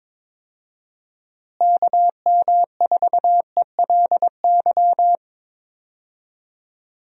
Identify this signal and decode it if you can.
Morse code, heard as a keyed tone, about 22 wpm, 700 Hz. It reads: KM4ELY
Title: USA Call Signs - 6 Characters - Repeated 2x - 22wpm